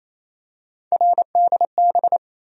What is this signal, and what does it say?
Morse code, heard as a keyed tone, about 28 wpm, 700 Hz. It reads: RDB
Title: USA Call Sign Postfixes - 3 Letters 28wpm